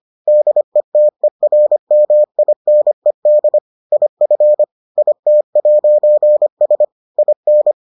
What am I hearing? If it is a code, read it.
Morse code, heard as a keyed tone, about 25 wpm, 600 Hz. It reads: DETERMINED IF IT'S IN